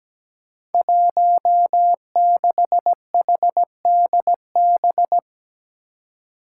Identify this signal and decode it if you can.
Morse code, heard as a keyed tone, about 17 wpm, 700 Hz. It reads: 16HDB
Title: Five Challenging Letter-Number Combinations 17wpm